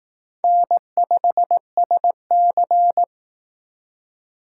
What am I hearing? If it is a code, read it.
Morse code, heard as a keyed tone, about 18 wpm, 700 Hz. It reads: N5SC
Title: Speed Racing - USA 1x2 Call Signs 15wpm